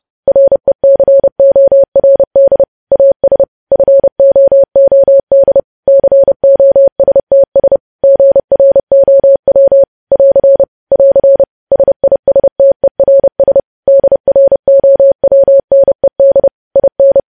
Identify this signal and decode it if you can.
Morse code, heard as a keyed tone, about 30 wpm, 550 Hz. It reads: RECORD AS FOOD COSTS GROW <AR> <AR> SISTERS DROWNED IN
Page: news headlines in morse code at 30 wpm